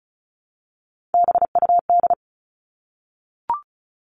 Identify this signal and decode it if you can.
Morse code, heard as a keyed tone, about 34 wpm, 700 Hz. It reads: BUD
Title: All Cut Numbers - 3-Digit 35wpm